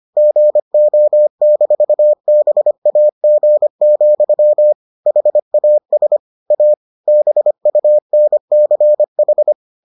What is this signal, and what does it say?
Morse code, heard as a keyed tone, about 25 wpm, 600 Hz. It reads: GO-BAG, HAS A BUNCH